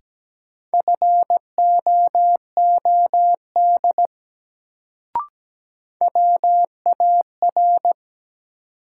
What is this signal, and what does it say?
Morse code, heard as a keyed tone, about 17 wpm, 700 Hz. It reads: FOOD WAR
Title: Sets of 5 Words - New Word Drill - 17wpm